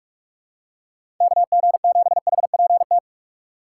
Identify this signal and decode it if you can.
Morse code, heard as a keyed tone, about 46 wpm, 700 Hz. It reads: KG7HPT
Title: USA Call Signs - 6 Characters - 3x Character Spacing - 45wpm